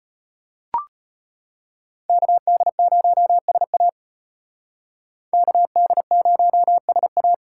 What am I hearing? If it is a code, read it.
Morse code, heard as a keyed tone, about 30 wpm, 700 Hz. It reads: KD0SA KD0SA
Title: Speed Racing - USA 2x2 Call Signs 25wpm